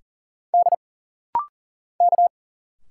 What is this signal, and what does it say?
Morse code, heard as a keyed tone, about 40 wpm, 700 Hz. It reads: D K
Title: Single Letters - Rapid-Fire 40wpm